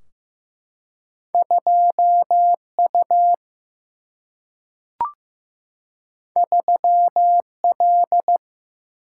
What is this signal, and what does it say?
Morse code, heard as a keyed tone, about 15 wpm, 700 Hz. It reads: 2 U 3 L
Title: Single-Letter Single-Number 15wpm